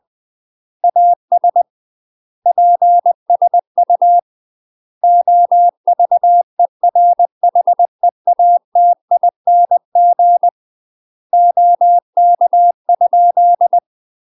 Morse code, heard as a keyed tone, about 20 wpm, 700 Hz. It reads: AS PSU OVERHEATING OK?
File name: UK MASTER QSOs TRAINING/20WPM